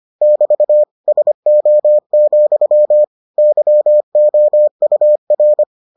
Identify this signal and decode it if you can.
Morse code, heard as a keyed tone, about 25 wpm, 600 Hz. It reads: <BT> SO, YOUR